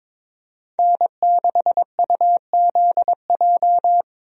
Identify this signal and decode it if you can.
Morse code, heard as a keyed tone, about 22 wpm, 700 Hz. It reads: N6UZJ
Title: USA Call Signs - 5 Characters - Repeated 3x - 22wpm